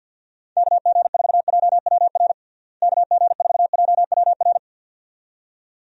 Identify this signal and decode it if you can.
Morse code, heard as a keyed tone, about 51 wpm, 700 Hz. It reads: KG4JWR KG4JWR
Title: USA Call Signs - 6 Characters - Repeated 2x - 50wpm